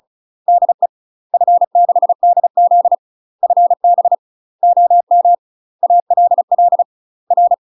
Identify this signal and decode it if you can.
Morse code, heard as a keyed tone, about 34 wpm, 700 Hz. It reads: DE F6DZ FB OM ALL R
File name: UK MASTER QSOs TRAINING / 35WPM / UK MASTER QSOs 0001